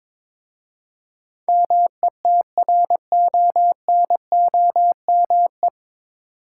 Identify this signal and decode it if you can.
Morse code, heard as a keyed tone, about 22 wpm, 700 Hz. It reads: METRONOME
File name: Difficult Words with Hints - 22wpm